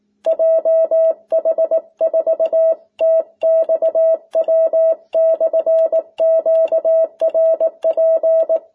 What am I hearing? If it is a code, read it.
Morse code, heard as a keyed tone, about 18 wpm, 650 Hz. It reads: JH4TXW/QRP
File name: jh8cbh / 2021cw